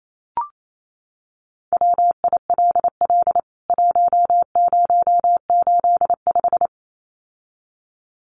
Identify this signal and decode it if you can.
Morse code, heard as a keyed tone, about 28 wpm, 700 Hz. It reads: WILL 1085
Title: CWT Exchange 28wpm